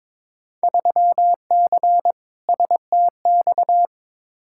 Morse code, heard as a keyed tone, about 22 wpm, 700 Hz. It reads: 3C STX
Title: Field Day Exchanges Encore - 22wpm